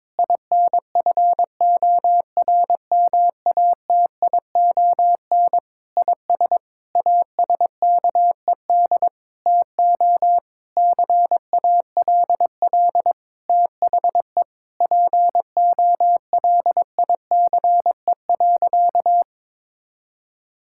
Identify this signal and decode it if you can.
Morse code, heard as a keyed tone, about 22 wpm, 700 Hz. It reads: INFORMATION IS ASKED TO CALL THE POLICE.
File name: Sentences with 7-Word Phrases - 22wpm